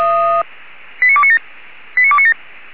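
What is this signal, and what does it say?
Five-tone selective call (ZVEI), heard as quick runs of five tones, tones of about 70 ms. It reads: 89297 89297
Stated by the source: ZVEI five-tone selective call received on SDR, from the Feuerwehr Orlamuende audio archive.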